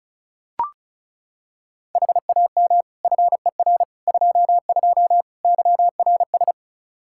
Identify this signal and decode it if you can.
Morse code, heard as a keyed tone, about 36 wpm, 700 Hz. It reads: HAM FER 22 YRS
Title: Practice QSO - Exchange 3 - More Info - 35wpm